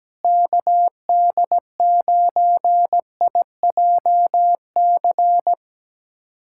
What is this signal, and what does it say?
Morse code, heard as a keyed tone, about 17 wpm, 700 Hz. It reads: KD9IJC
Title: USA Callsign DE Callsign - 6 Characters - 17wpm